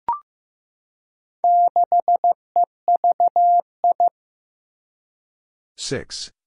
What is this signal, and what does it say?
Morse code, heard as a keyed tone, about 15 wpm, 700 Hz. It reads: 6EVI
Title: Four Challenging Letter-Number Combinations 15wpm